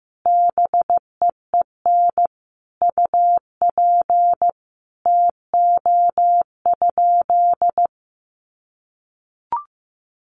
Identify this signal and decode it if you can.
Morse code, heard as a keyed tone, about 15 wpm, 700 Hz. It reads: BEEN UP TO?
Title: Sentences from Top 100 Words 15wpm